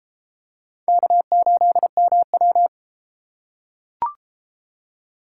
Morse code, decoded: K8MW

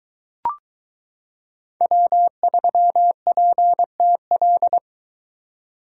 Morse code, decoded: W3PTL

